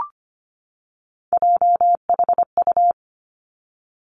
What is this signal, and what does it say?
Morse code, heard as a keyed tone, about 25 wpm, 700 Hz. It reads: JHU
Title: Three Challenging Letter-Number Combinations 25wpm